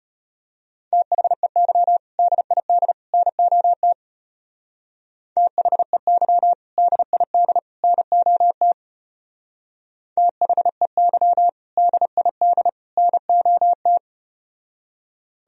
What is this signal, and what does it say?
Morse code, heard as a keyed tone, about 34 wpm, 700 Hz. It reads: THEY DID NOT THEY DID NOT THEY DID NOT